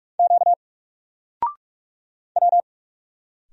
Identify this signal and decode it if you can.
Morse code, heard as a keyed tone, about 46 wpm, 700 Hz. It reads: Q W
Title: Single Letter-Number 45wpm